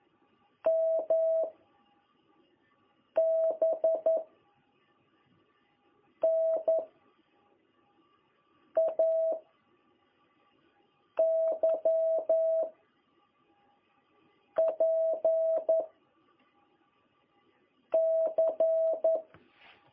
Morse code, heard as a keyed tone, about 11 wpm, 660 Hz. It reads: M B N A Y P C
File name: jh8cbh / 2021cw